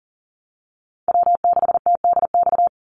Morse code, heard as a keyed tone, about 40 wpm, 700 Hz. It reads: W6TDX
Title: NA SOTA Chaser Callsigns 40wpm